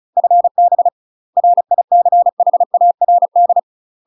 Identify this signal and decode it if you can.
Morse code, heard as a keyed tone, about 34 wpm, 700 Hz. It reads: FB RICHARD
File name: UK MASTER QSOs TRAINING/35WPM